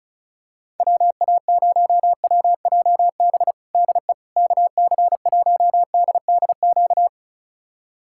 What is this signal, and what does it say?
Morse code, heard as a keyed tone, about 36 wpm, 700 Hz. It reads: WA0WJB DE KC1DDQ
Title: USA Callsign DE Callsign - 6 Characters - 35wpm